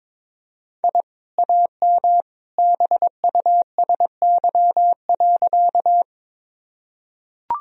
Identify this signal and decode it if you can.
Morse code, heard as a keyed tone, about 22 wpm, 700 Hz. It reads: I AM BUSY.
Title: Sentences from Top 1000 Words - Review Part 3 - 22wpm